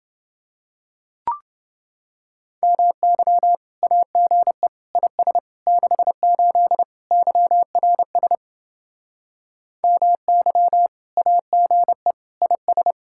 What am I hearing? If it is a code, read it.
Morse code, heard as a keyed tone, about 30 wpm, 700 Hz. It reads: MY AGE IS 68 YRS MY AGE IS